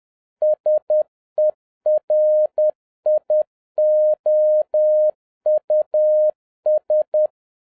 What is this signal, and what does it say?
Morse code, heard as a keyed tone, about 10 wpm, 610 Hz. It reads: SERIOUS
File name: 2011-07-mp3 / n56 BACK FROM A BIG ACCIDENT 10wpm